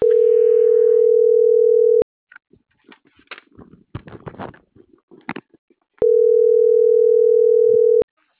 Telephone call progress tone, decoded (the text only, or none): ringback tone